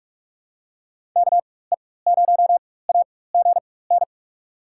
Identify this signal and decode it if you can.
Morse code, heard as a keyed tone, about 41 wpm, spaced as for 20 wpm, 700 Hz. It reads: KE0AGN
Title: USA Call Signs - 6 Characters - 4x Character Spacing - 45wpm